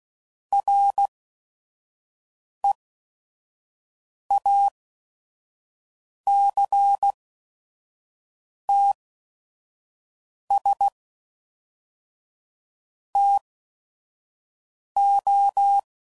Morse code, heard as a keyed tone, about 16 wpm, spaced as for 5 wpm, 800 Hz. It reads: REACTS TO
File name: CW Code Groups @5WPM / Technical Subjects / Reactance